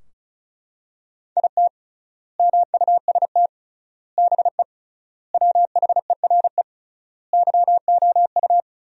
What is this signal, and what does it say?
Morse code, heard as a keyed tone, about 36 wpm, 700 Hz. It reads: IT MUST BE WHERE YOU
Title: 3x Word-Spacing - Sentences from Top 200 Words - 35wpm